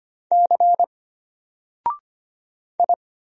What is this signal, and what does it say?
Morse code, heard as a keyed tone, about 25 wpm, 700 Hz. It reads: C I
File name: Single-Letter-Number-25wpm